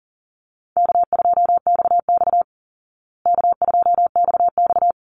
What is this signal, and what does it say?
Morse code, heard as a keyed tone, about 40 wpm, 700 Hz. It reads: K2XX K2XX
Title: Call Signs Repeated 3x - 40wpm